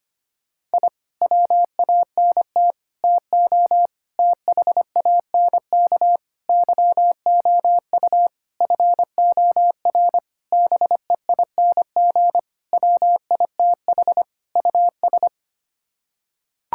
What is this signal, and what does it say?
Morse code, heard as a keyed tone, about 25 wpm, 700 Hz. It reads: I WANT TO THANK YOU FOR BEING WITH US
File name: Sets of 9 Words - 25wpm